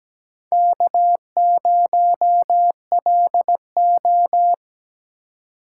Morse code, heard as a keyed tone, about 17 wpm, 700 Hz. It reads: K0LO
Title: USA Call Signs - 4 Characters - 2x Character Spacing - 17wpm